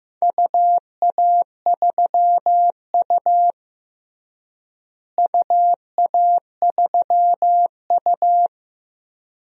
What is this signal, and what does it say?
Morse code, heard as a keyed tone, about 15 wpm, 700 Hz. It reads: UA3U UA3U
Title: DXCC-100 Call Signs - 4 Characters - Repeated 2x - 15wpm